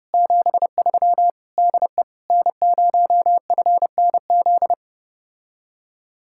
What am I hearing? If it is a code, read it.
Morse code, heard as a keyed tone, about 31 wpm, 700 Hz. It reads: 73 DE N0FNZ